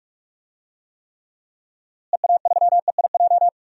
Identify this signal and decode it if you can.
Morse code, heard as a keyed tone, about 43 wpm, 700 Hz. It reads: EA3EIJ
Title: DXCC-100 Call Signs - 6 Characters - Repeated 3x - 45wpm